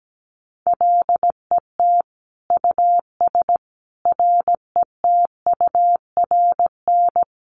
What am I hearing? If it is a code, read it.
Morse code, heard as a keyed tone, about 17 wpm, 700 Hz. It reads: LET US RETURN
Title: Sentences from Top 800 Words 17wpm